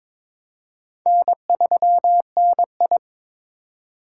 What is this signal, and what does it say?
Morse code, heard as a keyed tone, about 22 wpm, 700 Hz. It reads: N3NI
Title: USA Call Signs - 4 Characters - 2x Character Spacing - 22wpm